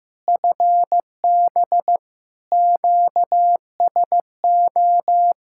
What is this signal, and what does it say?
Morse code, heard as a keyed tone, about 16 wpm, 700 Hz. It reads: FB QSO
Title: K1USN SST QSO Practice 15wpm